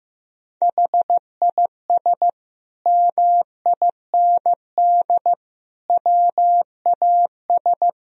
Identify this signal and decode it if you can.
Morse code, heard as a keyed tone, about 15 wpm, 700 Hz. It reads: HIS MIND WAS
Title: Sentences from Top 1000 Words - Review Part 1 - 15wpm